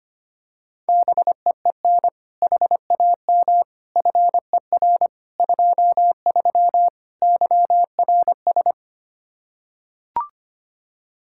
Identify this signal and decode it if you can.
Morse code, heard as a keyed tone, about 25 wpm, 700 Hz. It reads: BEEN HAM FER 23 YRS
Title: Practice QSO - Exchange 3 - More Info - 25wpm